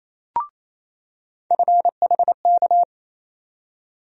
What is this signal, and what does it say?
Morse code, heard as a keyed tone, about 27 wpm, 700 Hz. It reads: FHK
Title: USA Call Sign Postfixes - 3 Letters 28wpm